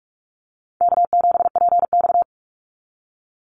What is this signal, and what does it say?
Morse code, heard as a keyed tone, about 46 wpm, 700 Hz. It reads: K7PX
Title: NA SOTA Chaser Callsigns 45wpm